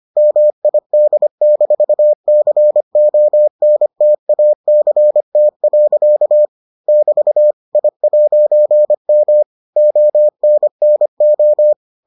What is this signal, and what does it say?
Morse code, heard as a keyed tone, about 25 wpm, 600 Hz. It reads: MID-CONTACT. <BT> I'M ONNO